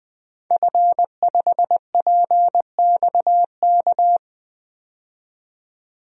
Morse code, heard as a keyed tone, about 21 wpm, 700 Hz. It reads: F5PXK
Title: DXCC-100 Call Signs - 5 Characters - 20wpm